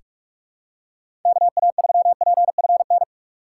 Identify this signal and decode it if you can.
Morse code, heard as a keyed tone, about 43 wpm, 700 Hz. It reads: KA3PFN
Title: USA Call Signs - 6 Characters - 3x Character Spacing - 45wpm